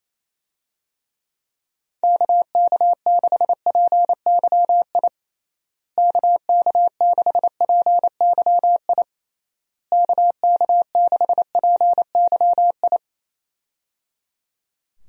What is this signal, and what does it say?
Morse code, heard as a keyed tone, about 27 wpm, 700 Hz. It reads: KK6PYI KK6PYI KK6PYI
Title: USA Call Signs - 6 Characters - Repeated 3x - 28wpm